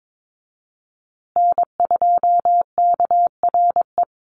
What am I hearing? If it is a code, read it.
Morse code, heard as a keyed tone, about 22 wpm, 700 Hz. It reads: N2KRE
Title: USA Call Signs Repeated 2x 22wpm